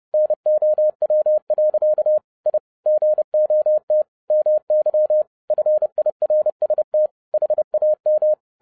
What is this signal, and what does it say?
Morse code, heard as a keyed tone, about 30 wpm, 610 Hz. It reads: NOW. I GOT MY FIRST HAM
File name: 2012-01-mp3 / n11-I DIDNT MISS THE SOLAR CYCLE 24-30wpm